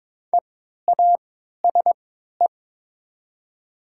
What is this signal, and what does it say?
Morse code, heard as a keyed tone, about 22 wpm, spaced as for 12 wpm, 700 Hz. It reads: EASE
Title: 3x Character Spacing - Top 300 Words - 22wpm